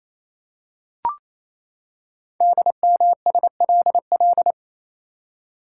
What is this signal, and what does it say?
Morse code, heard as a keyed tone, about 27 wpm, 700 Hz. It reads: DMSLL
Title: Code Groups - 5 Letter:Numbers - 28wpm